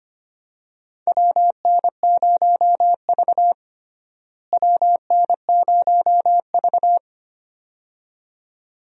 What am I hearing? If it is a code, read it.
Morse code, heard as a keyed tone, about 25 wpm, 700 Hz. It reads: WN0V WN0V
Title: USA Call Signs - 4 Characters - Repeated 2x - 25wpm